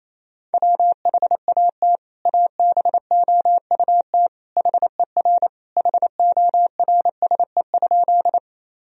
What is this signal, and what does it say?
Morse code, heard as a keyed tone, about 28 wpm, 700 Hz. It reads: WHAT ABOUT HER HORSE?
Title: Sentences from Top 1000 Words - Review Part 1 - 28wpm